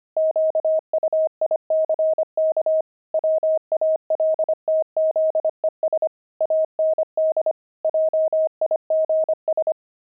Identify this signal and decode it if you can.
Morse code, heard as a keyed tone, about 25 wpm, 630 Hz. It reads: QUICK WALTZES AND JIGS
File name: pangram-fs-25-cs-25-heavy